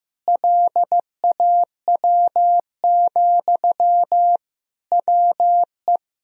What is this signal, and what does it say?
Morse code, heard as a keyed tone, about 15 wpm, 700 Hz. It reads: LAW, WE
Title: Sentences from Top 1000 Words - Review Part 4 - 15wpm